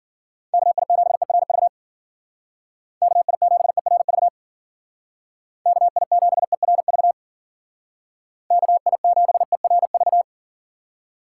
Morse code, heard as a keyed tone, about 50 wpm, 700 Hz. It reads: KI7ERV KI7ERV KI7ERV KI7ERV
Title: Speed Racing - USA 2x3 Call Signs 40wpm